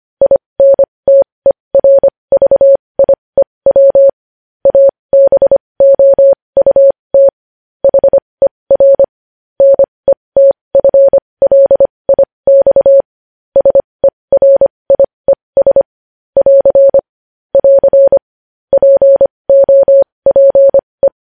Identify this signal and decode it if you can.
Morse code, heard as a keyed tone, about 25 wpm, spaced as for 20 wpm, 550 Hz. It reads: INTERVIEW ABOUT HER NETFLIX SERIES <AR> <AR> POPE
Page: news-headlines-in-morse-code-at-20-wpm